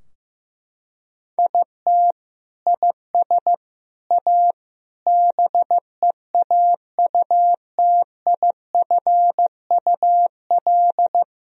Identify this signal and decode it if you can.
Morse code, heard as a keyed tone, about 15 wpm, 700 Hz. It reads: IT IS A BEAUTIFUL
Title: Sentences from Top 1000 Words - Review Part 2 - 15wpm